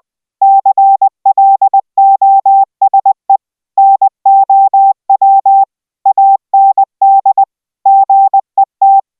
Morse code, heard as a keyed tone, about 20 wpm, 780 Hz. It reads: CLOSE NOW AND GET